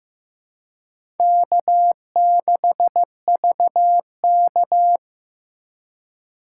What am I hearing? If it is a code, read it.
Morse code, heard as a keyed tone, about 16 wpm, 700 Hz. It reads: K6VK
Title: USA Call Signs - 4 Characters - 3x Character Spacing - 15wpm